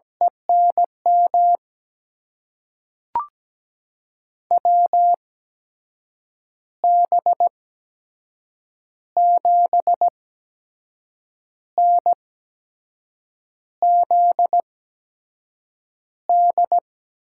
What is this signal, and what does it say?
Morse code, heard as a keyed tone, about 17 wpm, spaced as for 6 wpm, 700 Hz. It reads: ENM WB7NZD
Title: USA Call Signs - 6 Characters - 8x Character Spacing - 17wpm